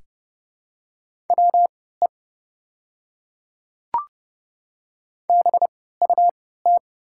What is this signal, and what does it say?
Morse code, heard as a keyed tone, about 30 wpm, spaced as for 17 wpm, 700 Hz. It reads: WE BUT
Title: 3x Character Spacing - Top 100 Words - 30wpm